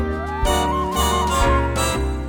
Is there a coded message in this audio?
No.